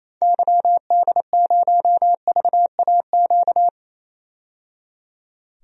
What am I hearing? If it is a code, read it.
Morse code, heard as a keyed tone, about 28 wpm, 700 Hz. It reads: YD0VAQ
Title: DXCC-100 Call Signs - 6 Characters - 28wpm